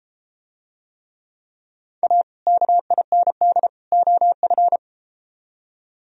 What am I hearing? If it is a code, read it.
Morse code, heard as a keyed tone, about 32 wpm, 700 Hz. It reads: A KIND OF